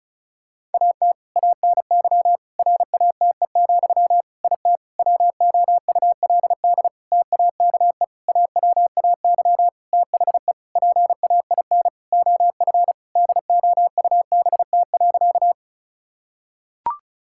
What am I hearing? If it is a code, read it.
Morse code, heard as a keyed tone, about 34 wpm, 700 Hz. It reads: AT ANY RATE, IT WOULD TAKE AWAY THE PAIN OF DOUBT.
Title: Sentences from Top 1600 Words 35wpm